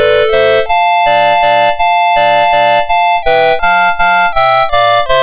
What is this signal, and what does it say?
There is no code.